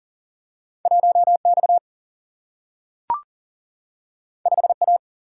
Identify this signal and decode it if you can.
Morse code, heard as a keyed tone, about 39 wpm, 700 Hz. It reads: 1X 5A
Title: Single-Letter Single-Number 40wpm